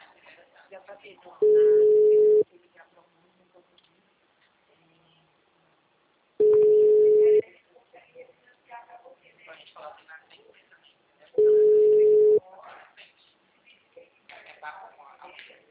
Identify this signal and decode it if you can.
Ringback tone (Europe, 425 Hz), with speech.
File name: out- -2009- - - .82825